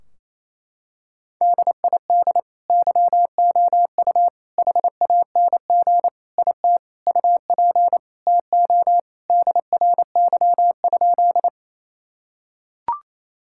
Morse code, heard as a keyed tone, about 28 wpm, 700 Hz. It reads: DID YOU HANG IT UP TO DRY?
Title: Sentences from Top 1600 Words 28wpm